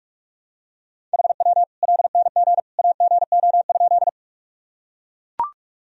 Morse code, heard as a keyed tone, about 42 wpm, 700 Hz. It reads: HW LNG AGO?
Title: Sentences from Top 500 Words and Abbreviations 45wpm